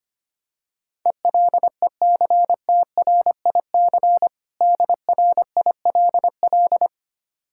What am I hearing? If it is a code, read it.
Morse code, heard as a keyed tone, about 25 wpm, 700 Hz. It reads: ELECTRIC DRILL
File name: practice / 25wpm / Difficult Words with Hints - 25wpm